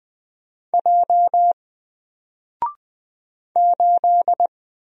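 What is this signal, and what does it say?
Morse code, heard as a keyed tone, about 20 wpm, 700 Hz. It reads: J 8
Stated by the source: Single letter and number Morse drill at 20 wpm, from N4LCD.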